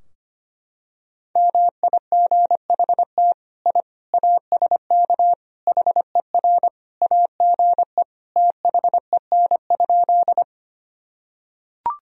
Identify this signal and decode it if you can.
Morse code, heard as a keyed tone, about 25 wpm, 700 Hz. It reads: MIGHT I ASK HER AGE THEN?
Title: Sentences from Top 1000 Words - Review Part 3 - 25wpm